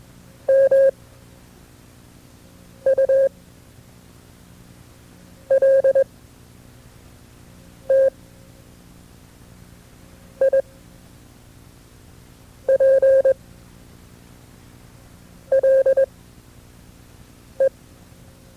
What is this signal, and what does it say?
Morse code, heard as a keyed tone, about 21 wpm, spaced as for 4 wpm, 540 Hz. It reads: MULTIPLE
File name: FreeFiles / CW-practice / KB6MT-tape2-side2